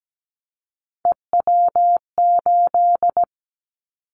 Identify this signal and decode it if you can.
Morse code, heard as a keyed tone, about 17 wpm, 700 Hz. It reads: EW8
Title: Callsign Prefixes for Top 50 DXCC - 17wpm